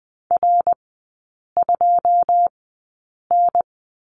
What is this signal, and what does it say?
Morse code, heard as a keyed tone, about 20 wpm, spaced as for 8 wpm, 690 Hz. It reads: R2N
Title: 145 Lesson 24e - Letter y - Three Characters 20wpm